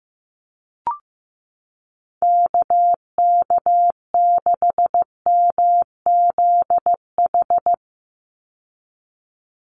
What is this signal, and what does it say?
Morse code, heard as a keyed tone, about 15 wpm, 700 Hz. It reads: KK6MZH
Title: USA Call Signs 15wpm